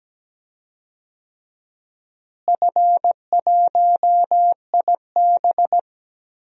Morse code, heard as a keyed tone, about 17 wpm, 700 Hz. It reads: F1IB
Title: DXCC-100 Call Signs - 4 Characters - Repeated 3x - 17wpm